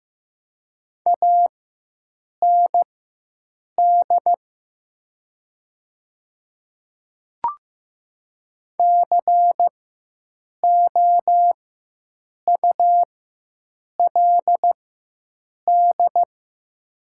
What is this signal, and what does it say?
Morse code, heard as a keyed tone, about 15 wpm, spaced as for 7 wpm, 700 Hz. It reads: AND COULD
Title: 4x Character Spacing - Top 100 Words - 15wpm